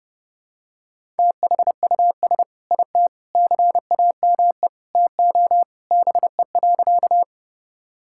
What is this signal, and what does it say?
Morse code, heard as a keyed tone, about 30 wpm, 700 Hz. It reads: THUS IT CAME TO BE.
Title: Sentences from Top 1000 Words - Review Part 5 - 30wpm